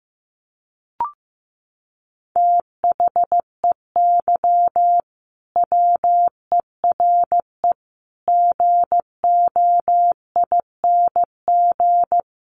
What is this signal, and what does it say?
Morse code, heard as a keyed tone, about 15 wpm, 700 Hz. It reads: THEY WERE GOING
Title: Sets of 4 Words 15wpm